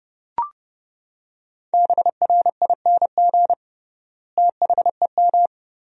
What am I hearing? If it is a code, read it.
Morse code, heard as a keyed tone, about 32 wpm, 700 Hz. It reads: BRING THEM
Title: 3x Word-Spacing - Sentences from Top 500 Words - 30wpm